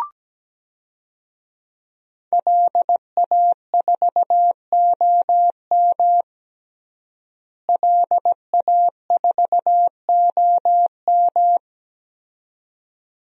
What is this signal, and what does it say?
Morse code, heard as a keyed tone, about 17 wpm, 700 Hz. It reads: LA4OM LA4OM
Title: DXCC-100 Call Signs - 5 Characters - Repeated 2x - 17wpm